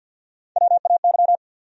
Morse code, heard as a keyed tone, about 52 wpm, 700 Hz. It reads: WAY
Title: Top 100 Words 50wpm